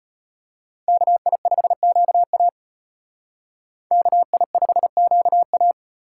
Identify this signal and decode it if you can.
Morse code, heard as a keyed tone, about 32 wpm, 700 Hz. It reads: KI5QA KI5QA